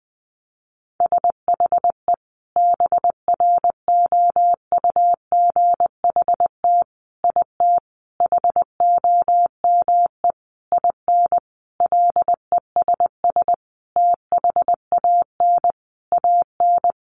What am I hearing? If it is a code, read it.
Morse code, heard as a keyed tone, about 20 wpm, 700 Hz. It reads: SHE BROUGHT IT HOME IN LESS THAN AN
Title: Sentences from Top 500 Words 20wpm